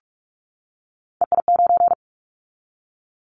Morse code, heard as a keyed tone, about 44 wpm, 700 Hz. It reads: EI9
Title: Callsign Prefixes for Top 50 DXCC - 45wpm